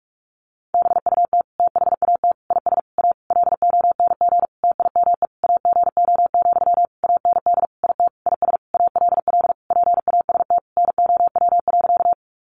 Morse code, heard as a keyed tone, about 44 wpm, 700 Hz. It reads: BUT THAT IS A LONG TIME AGO, AND IT IS ALL PAST NOW.
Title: Sentences from Top 600 Words 45wpm